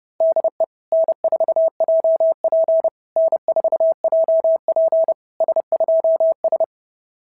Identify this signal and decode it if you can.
Morse code, heard as a keyed tone, about 28 wpm, 690 Hz. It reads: DE N4JP N4JP S2S
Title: SOTA QSO Practice 30wpm